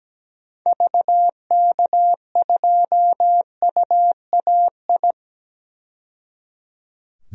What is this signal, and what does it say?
Morse code, heard as a keyed tone, about 17 wpm, 700 Hz. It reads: VK2UAI
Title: DXCC-100 Call Signs - 6 Characters - 17wpm